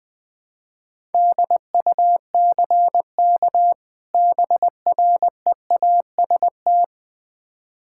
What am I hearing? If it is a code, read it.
Morse code, heard as a keyed tone, about 21 wpm, 700 Hz. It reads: DUCK BREAST
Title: Difficult Words with Hints - 20wpm